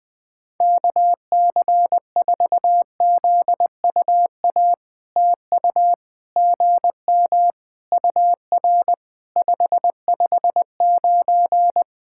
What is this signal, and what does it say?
Morse code, heard as a keyed tone, about 20 wpm, 700 Hz. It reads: KC4ZUA TU GM UR 559
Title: POTA QSO Practice - 20wpm